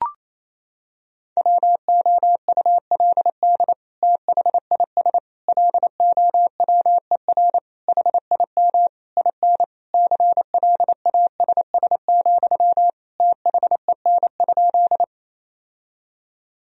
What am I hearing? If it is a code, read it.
Morse code, heard as a keyed tone, about 28 wpm, 700 Hz. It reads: WOULD THIS LOWER HIM IN CLASS, THEN?
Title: Sentences from Top 1000 Words - Review Part 4 - 28wpm